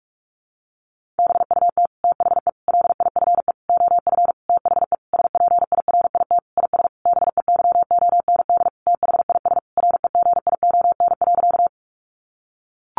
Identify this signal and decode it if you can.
Morse code, heard as a keyed tone, about 44 wpm, 700 Hz. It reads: BUT THE LIFE OF THE SPIRIT IS BEYOND THIS REGION.
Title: Sentences from Top 1500 Words 45wpm